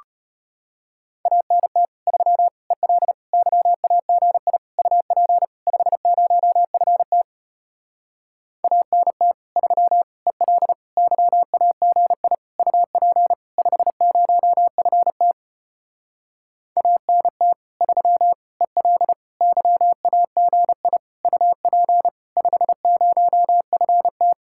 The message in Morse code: ANT 3 EL YAGI UP 50FT ANT 3 EL YAGI UP 50FT ANT 3 EL YAGI UP 50FT

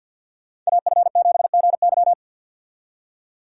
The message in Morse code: AF7GY

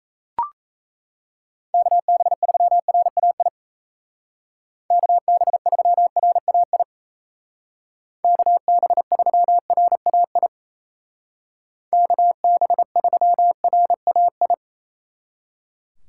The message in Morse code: KB3RAI KB3RAI KB3RAI KB3RAI

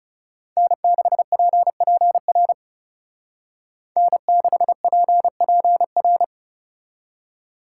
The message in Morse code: N6PPR N6PPR

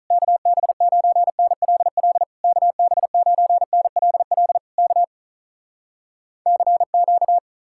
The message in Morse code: KB9NLL KB9NLL K CQ

